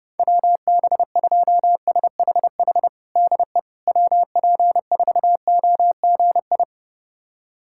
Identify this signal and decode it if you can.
Morse code, heard as a keyed tone, about 30 wpm, 700 Hz. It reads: WB2SHH DE WP4OGI
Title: USA Callsign DE Callsign - 6 Characters - 30wpm